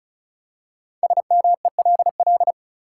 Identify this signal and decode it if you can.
Morse code, heard as a keyed tone, about 36 wpm, 700 Hz. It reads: SMELL
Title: Five Letter Words - 35wpm